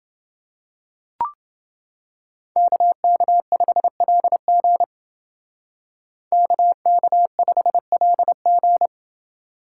Morse code, heard as a keyed tone, about 28 wpm, 700 Hz. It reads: KK5LG KK5LG